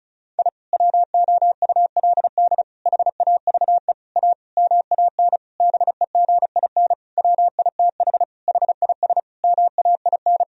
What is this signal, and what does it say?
Morse code, heard as a keyed tone, about 36 wpm, 700 Hz. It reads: I WOULD HAVE A MAN BEGIN WITH HIS MAIN
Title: Sentences from Top 1000 Words - Review Part 2 - 35wpm